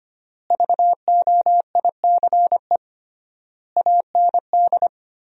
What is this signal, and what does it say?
Morse code, heard as a keyed tone, about 25 wpm, 700 Hz. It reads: VOICE AND
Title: 3x Word-Spacing - Sentences from Top 500 Words - 25wpm